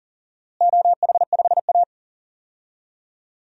Morse code, heard as a keyed tone, about 40 wpm, 700 Hz. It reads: OH5A
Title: DXCC-100 Call Signs - 4 Characters - Repeated 3x - 40wpm